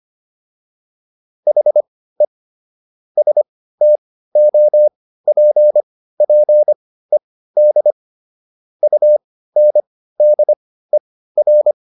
Morse code, heard as a keyed tone, about 25 wpm, spaced as for 15 wpm, 600 Hz. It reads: HE STOPPED UNDER